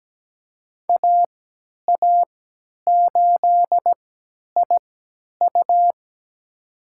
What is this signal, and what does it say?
Morse code, heard as a keyed tone, about 17 wpm, spaced as for 10 wpm, 700 Hz. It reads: AA8IU
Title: USA Call Signs - 5 Characters - 3x Character Spacing - 17wpm